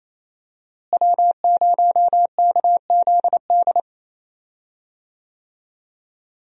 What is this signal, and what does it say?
Morse code, heard as a keyed tone, about 28 wpm, 700 Hz. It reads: W0KZD
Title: USA Call Signs - 5 Characters - Repeated 2x - 28wpm